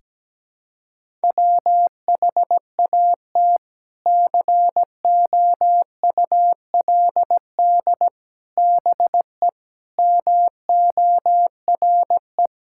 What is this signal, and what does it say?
Morse code, heard as a keyed tone, about 17 wpm, 700 Hz. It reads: WHAT COULD BE MORE
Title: Sentences from Top 1000 Words - Review Part 6 - 17wpm